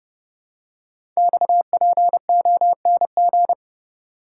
Morse code, heard as a keyed tone, about 30 wpm, 700 Hz. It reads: XPONG
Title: Code Groups - 5 Letters - 30wpm